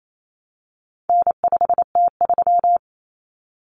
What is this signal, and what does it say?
Morse code, heard as a keyed tone, about 28 wpm, 700 Hz. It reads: N5T3
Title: Common Cut Numbers - 4-Digit 28wpm